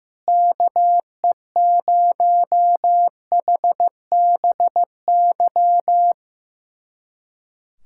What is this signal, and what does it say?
Morse code, heard as a keyed tone, about 16 wpm, 700 Hz. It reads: KE0HBY
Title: USA Callsign DE Callsign - 6 Characters - 15wpm